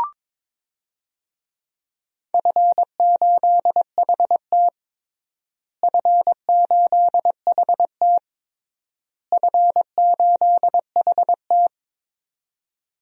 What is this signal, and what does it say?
Morse code, heard as a keyed tone, about 22 wpm, 700 Hz. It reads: F8HT F8HT F8HT
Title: DXCC-100 Call Signs - 4 Characters - Repeated 3x - 22wpm